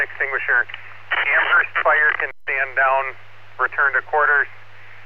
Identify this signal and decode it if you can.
Emergency Alert System (SAME) tones, heard as harsh buzzing bursts, with speech. It is heard from 1.2 to 4.1 s.